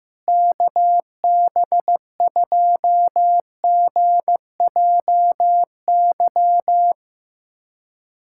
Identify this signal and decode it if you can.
Morse code, heard as a keyed tone, about 15 wpm, 700 Hz. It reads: KB2GJY
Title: USA Callsign DE Callsign - 6 Characters - 15wpm